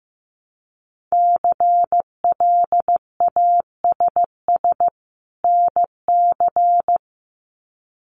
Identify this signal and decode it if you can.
Morse code, heard as a keyed tone, about 15 wpm, 700 Hz. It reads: CLASS NC
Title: Winter Field Day Exchanges 15wpm